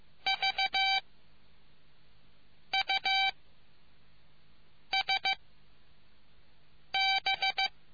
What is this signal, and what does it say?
Morse code, heard as a keyed tone, about 15 wpm, 770 Hz. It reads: V U S B